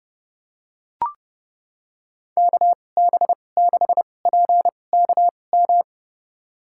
Morse code, heard as a keyed tone, about 30 wpm, spaced as for 22 wpm, 700 Hz. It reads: KB6PKM
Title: USA Call Signs - 6 Characters - 2x Character Spacing - 30wpm